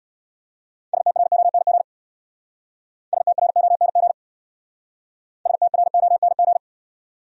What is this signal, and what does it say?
Morse code, heard as a keyed tone, about 70 wpm, 700 Hz. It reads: STRONG STRONG STRONG